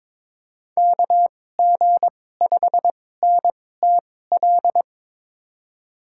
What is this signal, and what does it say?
Morse code, heard as a keyed tone, about 22 wpm, spaced as for 16 wpm, 700 Hz. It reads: KG5NTL
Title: USA Call Signs - 6 Characters - 2x Character Spacing - 22wpm